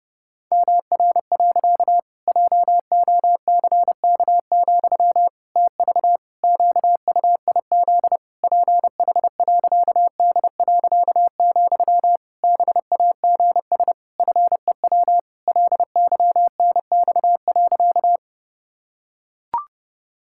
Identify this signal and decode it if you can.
Morse code, heard as a keyed tone, about 30 wpm, 700 Hz. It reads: MR. JOCK, TV QUIZ PH.D., BAGS FEW LYNX.
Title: Pangram Challenge v2 - 30wpm